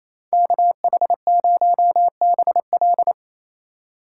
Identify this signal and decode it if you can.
Morse code, heard as a keyed tone, about 34 wpm, 700 Hz. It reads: KH0BL
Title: Speed Racing - USA 2x2 Call Signs 28wpm